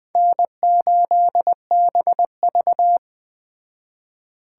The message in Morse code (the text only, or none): N8BV